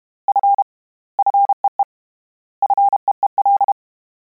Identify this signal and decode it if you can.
Morse code, heard as a keyed tone, about 32 wpm, 800 Hz. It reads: F FEE FEEL
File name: Combinations / Combinations-mp3s / Combinations@30